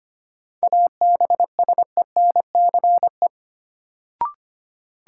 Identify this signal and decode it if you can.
Morse code, heard as a keyed tone, about 25 wpm, 700 Hz. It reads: ABSENCE
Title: Sets of 4 Words - New Word Drill - 25wpm